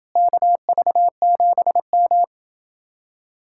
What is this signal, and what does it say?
Morse code, heard as a keyed tone, about 31 wpm, 700 Hz. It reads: KV7M